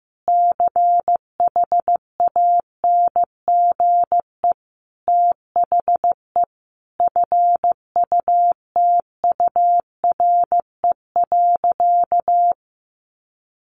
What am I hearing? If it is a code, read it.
Morse code, heard as a keyed tone, about 15 wpm, 700 Hz. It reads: CHANGE THE FUTURE.
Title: A Sound Of Thunder 15wpm